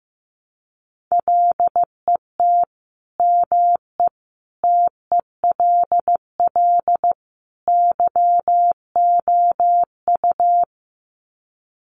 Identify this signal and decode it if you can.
Morse code, heard as a keyed tone, about 15 wpm, 700 Hz. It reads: LET ME TELL YOU
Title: Sets of 4 Words 15wpm